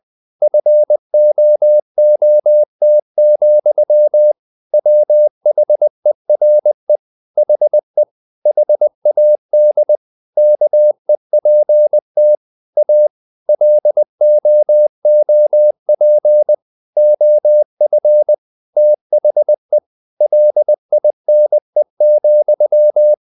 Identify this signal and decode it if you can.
Morse code, heard as a keyed tone, about 20 wpm, 600 Hz. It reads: FOOT, WHERE HE HAD KEPT A LOOP OF THE LINE,
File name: the old man and the sea / the old man and the sea part 1 20wpm